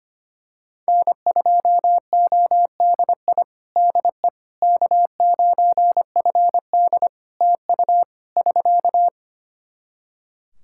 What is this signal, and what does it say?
Morse code, heard as a keyed tone, about 25 wpm, 700 Hz. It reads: N2ODI DE K9FD TU <SK>
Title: Practice QSO - Exchange 4 - Ending - 25wpm